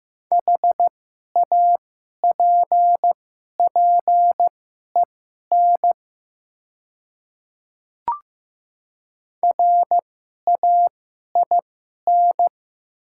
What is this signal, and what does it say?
Morse code, heard as a keyed tone, about 15 wpm, spaced as for 11 wpm, 700 Hz. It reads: HAPPEN RAIN
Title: 2x Character Spacing - Top 500 Words - 15wpm